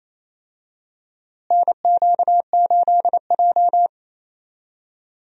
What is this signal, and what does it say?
Morse code, heard as a keyed tone, about 28 wpm, 700 Hz. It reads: NQ8J
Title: USA Call Signs - 4 Characters - Repeated 2x - 28wpm